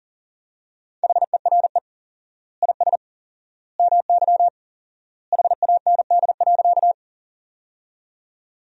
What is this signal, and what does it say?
Morse code, heard as a keyed tone, about 40 wpm, 700 Hz. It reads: HERE IS MY HAND.
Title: 4x Word-Spacing - Sentences from Top 200 Words - 40wpm